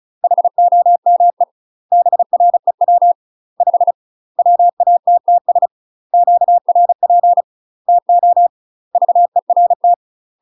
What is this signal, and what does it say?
Morse code, heard as a keyed tone, about 34 wpm, 700 Hz. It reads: HOME BREW 5 WATTS QRP TO VERT
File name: UK MASTER QSOs TRAINING / 35WPM / UK MASTER QSOs 0022